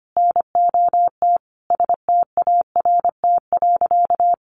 Morse code, heard as a keyed tone, about 25 wpm, 700 Hz. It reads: NOT START.
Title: Sentences from Top 1500 Words 25wpm